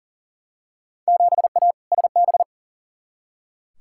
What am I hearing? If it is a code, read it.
Morse code, heard as a keyed tone, about 40 wpm, 700 Hz. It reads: 7A SB
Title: Field Day Exchanges Encore - 40wpm